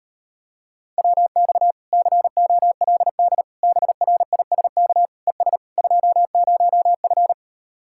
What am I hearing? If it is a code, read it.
Morse code, heard as a keyed tone, about 38 wpm, 700 Hz. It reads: WX COLD BRISK ES 20F